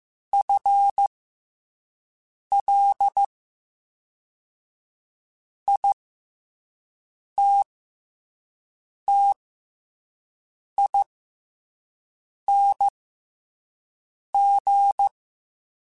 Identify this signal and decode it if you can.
Morse code, heard as a keyed tone, about 15 wpm, spaced as for 5 wpm, 800 Hz. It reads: FLITTING
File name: FL / FL mp3s / FL Prefixes 3x2 sp @5